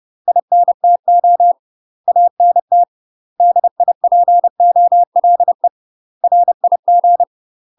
Morse code, heard as a keyed tone, about 30 wpm, 700 Hz. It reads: INTO ANT DIPOLE RIG
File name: UK MASTER QSOs TRAINING / 30WPM / UK MASTER QSOs 0001